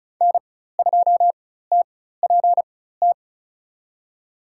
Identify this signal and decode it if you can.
Morse code, heard as a keyed tone, about 34 wpm, spaced as for 16 wpm, 700 Hz. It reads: N2TPT
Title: USA Call Signs - 4x Character Spacing - 35wpm